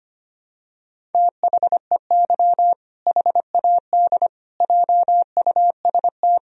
Morse code, heard as a keyed tone, about 25 wpm, 700 Hz. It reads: THEY HAD JUST
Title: Sentences from Top 1000 Words - Review Part 2 - 25wpm